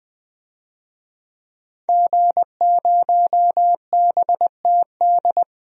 Morse code, heard as a keyed tone, about 20 wpm, 700 Hz. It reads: G0BTD
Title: DXCC-100 Call Signs - 5 Characters - 20wpm